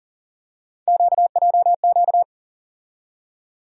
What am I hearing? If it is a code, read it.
Morse code, heard as a keyed tone, about 40 wpm, 700 Hz. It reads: QJQ
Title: USA Call Sign Postfixes - 3 Letters 40wpm